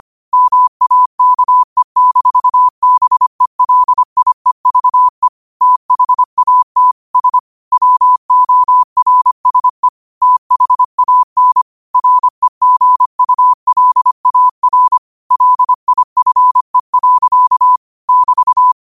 Morse code, heard as a keyed tone, about 25 wpm, 1000 Hz. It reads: MAKE-BELIEVE THAT S WORSE THAN REGULAR LIFE. <BT>
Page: Quotes for Fri, 15 Aug 2025 in Morse Code at 25 words per minute.